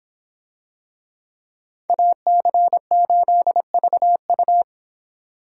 Morse code, heard as a keyed tone, about 24 wpm, 700 Hz. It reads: AC8VU